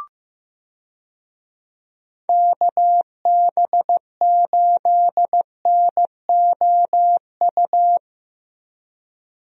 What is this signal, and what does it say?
Morse code, heard as a keyed tone, about 15 wpm, 700 Hz. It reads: KB8NOU
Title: USA Call Signs - 6 Characters - Repeated 2x - 15wpm